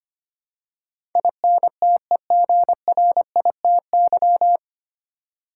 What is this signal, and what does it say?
Morse code, heard as a keyed tone, about 25 wpm, 700 Hz. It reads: INTEGRITY
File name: Nine Letter Words - 25wpm